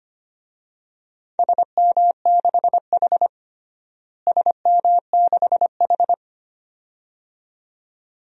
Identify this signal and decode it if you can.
Morse code, heard as a keyed tone, about 25 wpm, 700 Hz. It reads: SM6H SM6H
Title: DXCC-100 Call Signs - 4 Characters - Repeated 2x - 25wpm